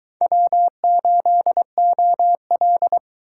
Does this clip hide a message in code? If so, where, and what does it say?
Yes: Morse code from 0.2 to 3.0 s, heard as a keyed tone, about 24 wpm, 700 Hz. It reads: W8OL